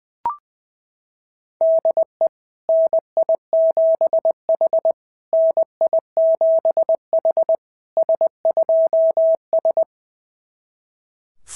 Morse code, heard as a keyed tone, about 18 wpm, 650 Hz. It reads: DE NI7H NI7H S2S
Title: SOTA QSO Practice 20wpm